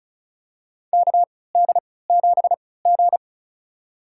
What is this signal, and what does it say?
Morse code, heard as a keyed tone, about 36 wpm, spaced as for 20 wpm, 700 Hz. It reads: KD7G
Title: USA Call Signs - 3x Character Spacing - 35wpm